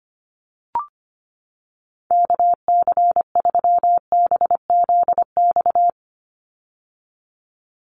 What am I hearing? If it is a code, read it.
Morse code, heard as a keyed tone, about 25 wpm, 700 Hz. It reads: KC3BZX
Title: USA Call Signs 25wpm